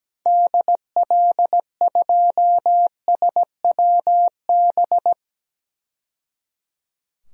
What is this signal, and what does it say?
Morse code, heard as a keyed tone, about 17 wpm, 700 Hz. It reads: DL2SWB
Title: DXCC-100 Call Signs - 6 Characters - 17wpm